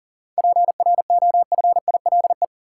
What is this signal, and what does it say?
Morse code, heard as a keyed tone, about 40 wpm, 700 Hz. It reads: PROFILE
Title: Seven Letter Words - 40wpm